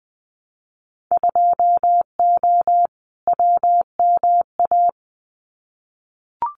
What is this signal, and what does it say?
Morse code, heard as a keyed tone, about 20 wpm, 700 Hz. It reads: 2O WMA
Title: Winter Field Day Exchanges Encore 20wpm